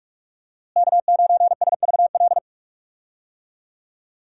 Morse code, heard as a keyed tone, about 42 wpm, 700 Hz. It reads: K9SVL
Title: USA Call Signs - 5 Characters - Repeated 3x - 45wpm